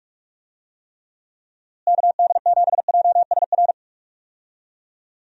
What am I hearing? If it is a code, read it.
Morse code, heard as a keyed tone, about 42 wpm, 700 Hz. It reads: KD7JSR
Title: USA Call Signs - 6 Characters - Repeated 3x - 45wpm